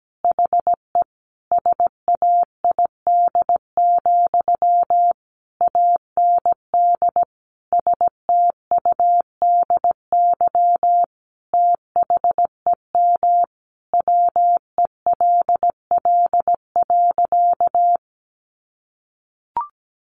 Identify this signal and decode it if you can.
Morse code, heard as a keyed tone, about 17 wpm, 700 Hz. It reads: HE SAID, AND STUDY THEM WELL.
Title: Sentences from Top 300 Words 17wpm